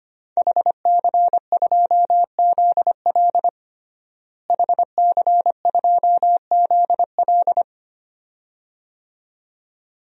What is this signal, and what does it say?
Morse code, heard as a keyed tone, about 25 wpm, 700 Hz. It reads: HC2ZL HC2ZL
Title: DXCC-100 Call Signs - 5 Characters - Repeated 2x - 25wpm